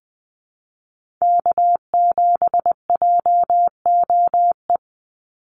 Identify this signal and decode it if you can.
Morse code, heard as a keyed tone, about 20 wpm, 700 Hz. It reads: K7JOE
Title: Call Signs - 20wpm